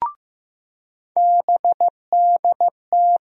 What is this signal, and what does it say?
Morse code, heard as a keyed tone, about 15 wpm, 700 Hz. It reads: BDT
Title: USA Call Sign Postfixes - 3 Letters 15wpm